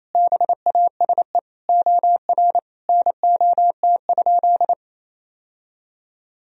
Morse code, heard as a keyed tone, about 27 wpm, 700 Hz. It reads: BASE OR NOT?
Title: Sentences from Top 1000 Words - Review Part 1 - 28wpm